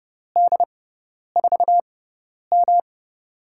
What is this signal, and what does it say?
Morse code, heard as a keyed tone, about 30 wpm, spaced as for 10 wpm, 700 Hz. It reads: D4M
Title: USA Call Signs - 6x Character Spacing - 30wpm